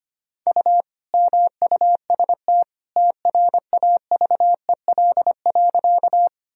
Morse code, heard as a keyed tone, about 25 wpm, 700 Hz. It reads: U MUST TRAVEL.
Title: Sentences from Top 400 Words and Abbreviations 25wpm